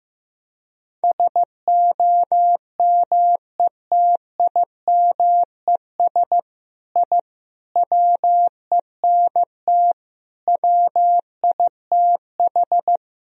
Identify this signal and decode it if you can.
Morse code, heard as a keyed tone, about 15 wpm, 700 Hz. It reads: SOMETIMES I WENT WITH
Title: Sentences from Top 1000 Words - Review Part 3 - 15wpm